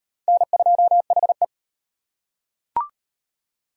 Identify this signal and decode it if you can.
Morse code, heard as a keyed tone, about 35 wpm, 700 Hz. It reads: N2HE